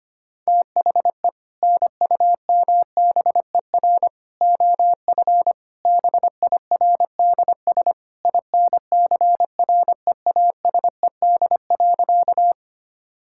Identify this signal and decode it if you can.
Morse code, heard as a keyed tone, about 25 wpm, 700 Hz. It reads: THE NUMBER OF BIRDS INCREASED.
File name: practice / 25wpm / Sentences with 3-Word Phrases - 25wpm